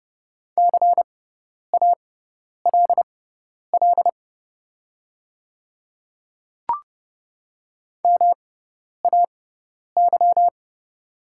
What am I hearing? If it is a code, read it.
Morse code, heard as a keyed tone, about 30 wpm, spaced as for 10 wpm, 700 Hz. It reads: CALL MAY
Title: 6x Character Spacing - Top 300 Words - 30wpm